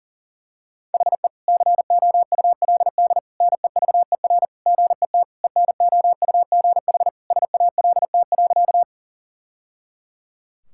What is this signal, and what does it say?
Morse code, heard as a keyed tone, about 40 wpm, 700 Hz. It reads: HE COULD NEVER GET ENOUGH SALT.
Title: Sentences from Top 1000 Words - Review Part 5 - 40wpm